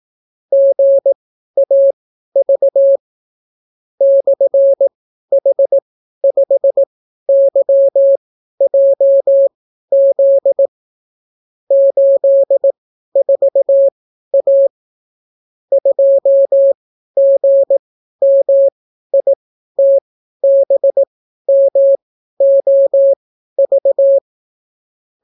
Morse code, heard as a keyed tone, about 18 wpm, spaced as for 12 wpm, 550 Hz. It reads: GAV /H5YJZ 84A 2GMITBMOV